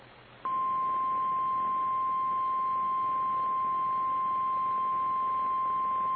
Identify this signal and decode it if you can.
Emergency Alert System (SAME) tones (the 1050 Hz attention tone). It is heard from 0.9 to 5.6 s.
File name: eas / KKPRKXPN / KKPRKXPN